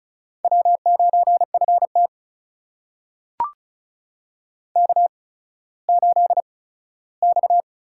Morse code, heard as a keyed tone, about 36 wpm, spaced as for 17 wpm, 700 Hz. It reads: W9FT K8X
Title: USA Call Signs - 4 Characters - 8x Character Spacing - 35wpm